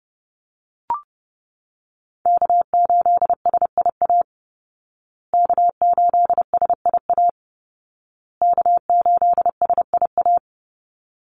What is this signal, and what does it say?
Morse code, heard as a keyed tone, about 30 wpm, 700 Hz. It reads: K8SIA K8SIA K8SIA
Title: Call Signs Repeated 2x - 30wpm